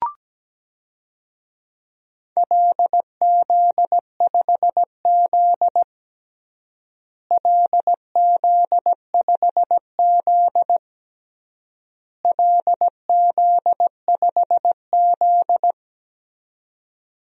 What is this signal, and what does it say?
Morse code, heard as a keyed tone, about 17 wpm, 700 Hz. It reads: LZ5Z LZ5Z LZ5Z
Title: DXCC-100 Call Signs - 4 Characters - Repeated 3x - 17wpm